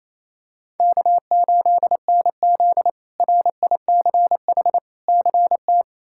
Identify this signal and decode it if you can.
Morse code, heard as a keyed tone, about 28 wpm, 700 Hz. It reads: K8NZ RICH CT
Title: K1USN SST QSO Practice - 28wpm